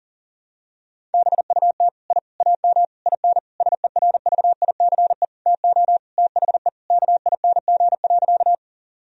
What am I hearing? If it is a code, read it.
Morse code, heard as a keyed tone, about 40 wpm, 700 Hz. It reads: BUT I AM IN SERVICE TO THE KING.
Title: Sentences from Top 1000 Words - Review Part 4 - 40wpm